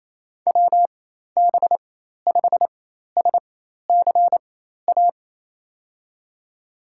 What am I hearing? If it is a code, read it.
Morse code, heard as a keyed tone, about 28 wpm, spaced as for 13 wpm, 700 Hz. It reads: WB5SCA
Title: USA Call Signs - 6 Characters - 4x Character Spacing - 28wpm